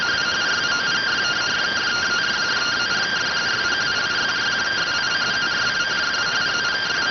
Two-tone teleprinter signal, about 170 Hz shift.